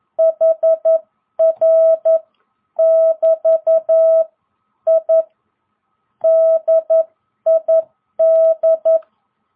Morse code, heard as a keyed tone, about 11 wpm, spaced as for 9 wpm, 660 Hz. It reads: HR<BT>I DID